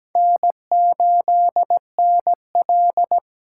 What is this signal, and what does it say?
Morse code, heard as a keyed tone, about 20 wpm, 700 Hz. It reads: N8NL
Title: Speed Racing - USA 1x2 Call Signs 17wpm